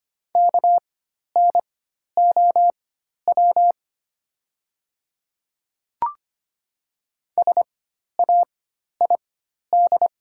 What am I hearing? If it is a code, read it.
Morse code, heard as a keyed tone, about 25 wpm, spaced as for 12 wpm, 700 Hz. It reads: KNOW SAID
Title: 4x Character Spacing - Top 100 Words - 25wpm